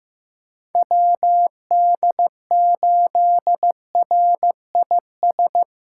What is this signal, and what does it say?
Morse code, heard as a keyed tone, about 15 wpm, 700 Hz. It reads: WD8RIS
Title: USA Callsign DE Callsign - 6 Characters - 15wpm